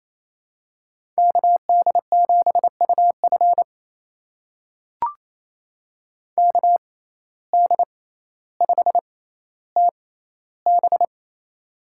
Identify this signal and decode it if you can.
Morse code, heard as a keyed tone, about 28 wpm, spaced as for 14 wpm, 700 Hz. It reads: KD7UF KD5TB
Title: USA Call Signs - 5 Characters - 6x Character Spacing - 28wpm